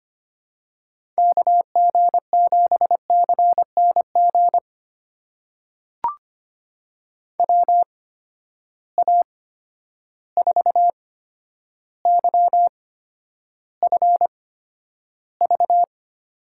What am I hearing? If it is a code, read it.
Morse code, heard as a keyed tone, about 25 wpm, spaced as for 11 wpm, 700 Hz. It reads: KG7CNG WA4YFV
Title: USA Call Signs - 6 Characters - 8x Character Spacing - 25wpm